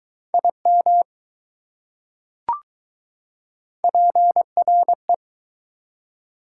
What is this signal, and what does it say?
Morse code, heard as a keyed tone, about 21 wpm, 700 Hz. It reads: IM PRE